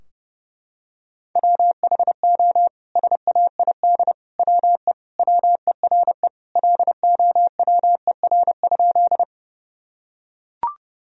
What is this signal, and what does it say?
Morse code, heard as a keyed tone, about 30 wpm, 700 Hz. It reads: WHO SAID WE WERE LOWER?
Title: Sentences from Top 1000 Words - Review Part 4 - 30wpm